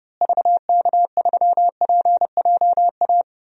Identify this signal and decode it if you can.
Morse code, heard as a keyed tone, about 32 wpm, 700 Hz. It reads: VK3PJA
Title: DXCC-100 Call Signs - 6 Characters - Repeated 2x - 30wpm